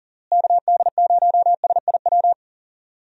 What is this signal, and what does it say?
Morse code, heard as a keyed tone, about 39 wpm, 700 Hz. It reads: KD0SIW
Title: USA Call Signs - 6 Characters - Repeated 2x - 40wpm